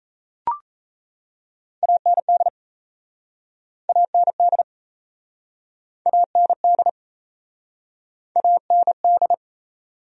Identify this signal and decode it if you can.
Morse code, heard as a keyed tone, about 37 wpm, 700 Hz. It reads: AND AND AND AND